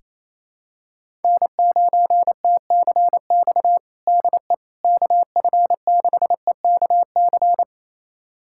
Morse code, heard as a keyed tone, about 28 wpm, 700 Hz. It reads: N9TCX DE KF6EKC
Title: USA Callsign DE Callsign 28wpm